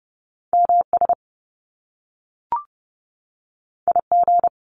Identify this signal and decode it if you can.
Morse code, heard as a keyed tone, about 30 wpm, 700 Hz. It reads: MS IG
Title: Top Two Letter Combinations in Words 30wpm